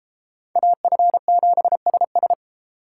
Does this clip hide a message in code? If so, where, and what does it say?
Yes: Morse code from 0.6 to 2.3 s, heard as a keyed tone, about 32 wpm, 700 Hz. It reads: AF7SS